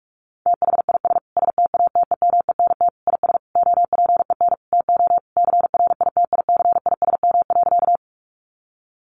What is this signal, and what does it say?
Morse code, heard as a keyed tone, about 44 wpm, 700 Hz. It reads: THIS STATEMENT IS OPEN TO CRITICISM.
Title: Sentences from Top 1300 Words 45wpm